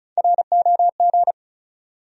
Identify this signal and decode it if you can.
Morse code, heard as a keyed tone, about 36 wpm, 700 Hz. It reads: ROG
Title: USA Common Names - Max 5 Letters - 35wpm